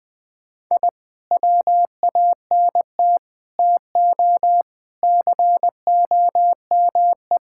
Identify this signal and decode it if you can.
Morse code, heard as a keyed tone, about 21 wpm, 700 Hz. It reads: I WANT TO COME
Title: Sentences from Top 1000 Words - Review Part 6 - 20wpm